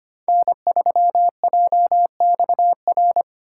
Morse code, heard as a keyed tone, about 25 wpm, 700 Hz. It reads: N3JXR
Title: USA Call Signs - 5 Characters - 3x Character Spacing - 25wpm